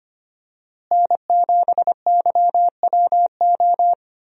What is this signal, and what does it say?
Morse code, heard as a keyed tone, about 25 wpm, 700 Hz. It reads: N7YWO
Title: USA Call Signs - 5 Characters - Repeated 3x - 25wpm